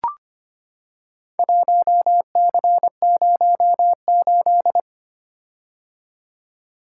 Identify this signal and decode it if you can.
Morse code, heard as a keyed tone, about 25 wpm, 700 Hz. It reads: 1C08
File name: practice / 25wpm / Four Letter-Number Combinations - 25wpm